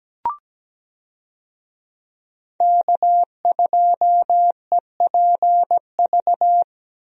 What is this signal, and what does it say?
Morse code, heard as a keyed tone, about 17 wpm, 700 Hz. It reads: K2EPV
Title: USA Call Signs - 5 Characters - 17wpm